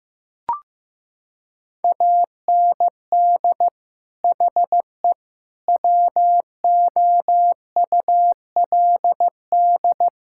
Morse code, heard as a keyed tone, about 14 wpm, 700 Hz. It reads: AND HE WOULD
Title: Sentences from Top 1600 Words 15wpm